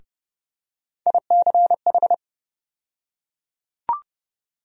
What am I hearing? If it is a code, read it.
Morse code, heard as a keyed tone, about 30 wpm, 700 Hz. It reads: ICH
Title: USA Call Sign Postfixes - 3 Letters 30wpm